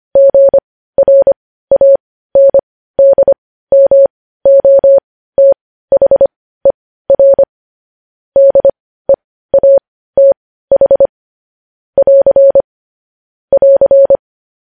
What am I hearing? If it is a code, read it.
Morse code, heard as a keyed tone, about 25 wpm, spaced as for 15 wpm, 550 Hz. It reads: GRANDMOTHER DEATH <AR> <AR>
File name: podcast / News headlines-1x-25x15